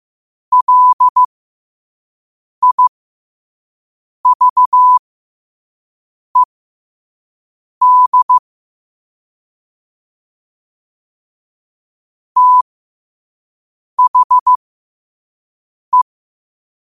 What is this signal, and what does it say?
Morse code, heard as a keyed tone, about 15 wpm, spaced as for 5 wpm, 1000 Hz. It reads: LIVED THE
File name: qotdfiles / 5wpm